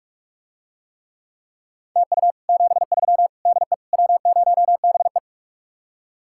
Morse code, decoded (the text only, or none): TU 73 DE W0BE